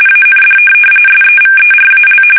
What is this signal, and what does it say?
RTTY, about 45.45 baud. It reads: AND SEVERAL
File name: fsk850 / USS-Massachusetts